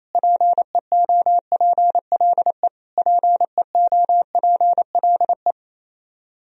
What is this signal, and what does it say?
Morse code, heard as a keyed tone, about 28 wpm, 700 Hz. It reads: PEOPLE PEOPLE
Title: Top 100 Words Repeated 28wpm